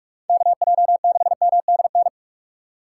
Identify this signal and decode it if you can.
Morse code, heard as a keyed tone, about 45 wpm, 700 Hz. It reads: KJ6MDN